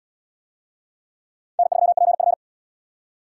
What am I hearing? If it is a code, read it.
Morse code, heard as a keyed tone, about 66 wpm, 700 Hz. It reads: N3PF